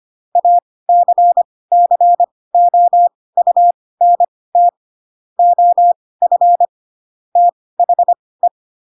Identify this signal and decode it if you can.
Morse code, heard as a keyed tone, about 25 wpm, spaced as for 18 wpm, 700 Hz. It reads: ACCOUNT OF THE
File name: Pinocchio / Pinocchio ch2 12 25 ik1hge